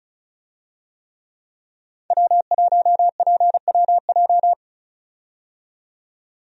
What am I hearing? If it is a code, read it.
Morse code, heard as a keyed tone, about 34 wpm, 700 Hz. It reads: W1PWJ
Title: USA Call Signs - 5 Characters - 35wpm